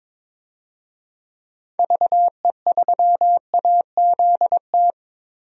Morse code, heard as a keyed tone, about 22 wpm, 700 Hz. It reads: VE3AZT
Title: DXCC-100 Call Signs - 6 Characters - 22wpm